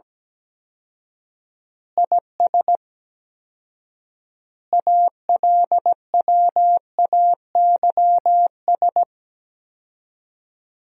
Morse code, heard as a keyed tone, about 17 wpm, 700 Hz. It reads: IS ALWAYS
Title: 4x Word-Spacing - Sentences from Top 400 Words - 17wpm